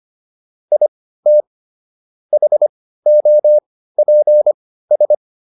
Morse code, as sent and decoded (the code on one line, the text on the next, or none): .. - / .... --- .--. ...
IT HOPS